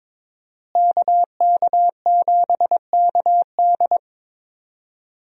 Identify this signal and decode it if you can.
Morse code, heard as a keyed tone, about 21 wpm, 700 Hz. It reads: KK7KD
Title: USA Call Signs - 5 Characters - Repeated 3x - 22wpm